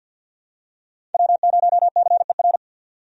Morse code, heard as a keyed tone, about 52 wpm, 700 Hz. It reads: W0CER
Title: USA Call Signs - 5 Characters - Repeated 3x - 50wpm